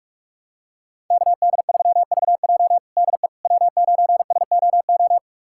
Morse code, heard as a keyed tone, about 42 wpm, 700 Hz. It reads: KD3VJ DE W9SOO
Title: USA Callsign DE Callsign - 5 Characters - 45wpm